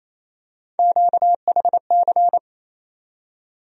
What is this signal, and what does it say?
Morse code, heard as a keyed tone, about 28 wpm, 700 Hz. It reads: QHC
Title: USA Call Sign Postfixes - 3 Letters 28wpm